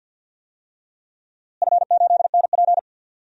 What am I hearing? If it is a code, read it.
Morse code, heard as a keyed tone, about 50 wpm, 700 Hz. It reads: F8NP
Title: DXCC-100 Call Signs - 4 Characters - Repeated 3x - 50wpm